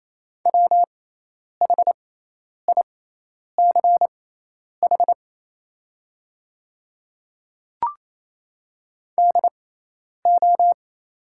Morse code, heard as a keyed tone, about 28 wpm, spaced as for 10 wpm, 700 Hz. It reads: WHICH DO
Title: 6x Character Spacing - Top 100 Words - 28wpm